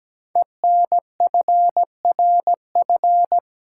Morse code, heard as a keyed tone, about 17 wpm, 700 Hz. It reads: ENFRF
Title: Code Groups - 5 Letters - 17wpm